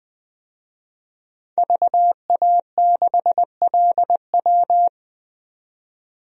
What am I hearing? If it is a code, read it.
Morse code, heard as a keyed tone, about 20 wpm, 700 Hz. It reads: VA6LW
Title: DXCC-100 Call Signs - 5 Characters - 20wpm